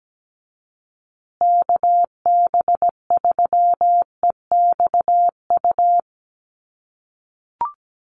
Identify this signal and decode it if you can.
Morse code, heard as a keyed tone, about 17 wpm, 700 Hz. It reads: KB3EXU
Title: USA Call Signs Repeated 2x 17wpm